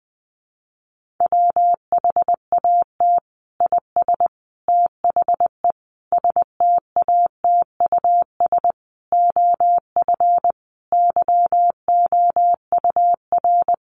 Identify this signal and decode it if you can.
Morse code, heard as a keyed tone, about 20 wpm, 700 Hz. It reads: WHAT IS THE STATUS OF YOUR
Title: Sentences from Top 1400 Words 20wpm